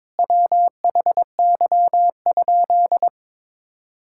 Morse code, heard as a keyed tone, about 21 wpm, 700 Hz. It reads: WHY?
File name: practice / 22wpm / Sentences with 4-Word Phrases - 22wpm